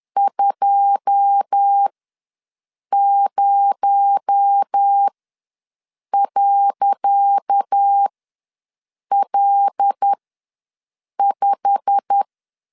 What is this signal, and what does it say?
Morse code, heard as a keyed tone, about 10 wpm, spaced as for 6 wpm, 800 Hz. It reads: 20.L5
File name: LEZIONI / 10 SIGNS / 10C DOT ws 10wpm 4213